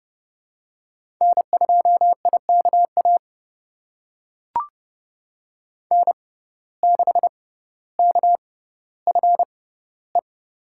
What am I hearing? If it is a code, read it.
Morse code, heard as a keyed tone, about 31 wpm, spaced as for 16 wpm, 700 Hz. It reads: N2IKA N6KFE
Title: USA Call Signs - 5 Characters - 6x Character Spacing - 30wpm